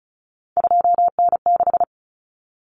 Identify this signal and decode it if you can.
Morse code, heard as a keyed tone, about 34 wpm, 700 Hz. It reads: 2N6
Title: Common Cut Numbers - 3-Digit 35wpm